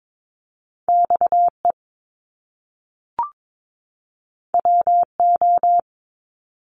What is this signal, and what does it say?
Morse code, heard as a keyed tone, about 22 wpm, 700 Hz. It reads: XE WO
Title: Top Two Letter Combinations in Words 22wpm